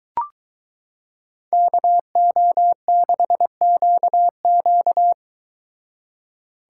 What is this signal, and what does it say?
Morse code, heard as a keyed tone, about 22 wpm, 700 Hz. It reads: KO6QQ